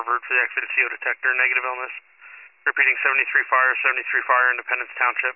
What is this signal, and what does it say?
Emergency Alert System (SAME) tones, heard as harsh buzzing bursts, with speech. It is heard from 0.9 to 5.0 s.
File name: pager / 73 FIRE / wcalert